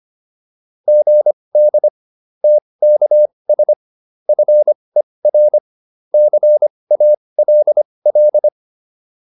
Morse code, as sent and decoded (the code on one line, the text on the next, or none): --. -.. / - -.- ... / ..-. .-. / -.-. .- .-.. .-..
GD TKS FER CALL